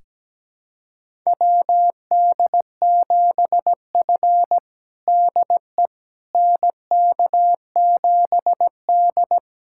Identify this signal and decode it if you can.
Morse code, heard as a keyed tone, about 17 wpm, 700 Hz. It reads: WD7F DE NK7D
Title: USA Callsign DE Callsign - 4 Characters - 17wpm